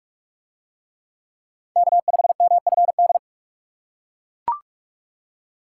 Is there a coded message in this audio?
Yes: Morse code from 1.8 to 3.2 s, heard as a keyed tone, about 44 wpm, 700 Hz. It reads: K5MFD